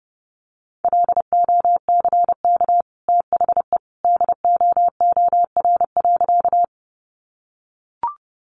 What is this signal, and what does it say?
Morse code, heard as a keyed tone, about 30 wpm, 700 Hz. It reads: LOCK THE DOOR.
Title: Sentences from Top 1300 Words 30wpm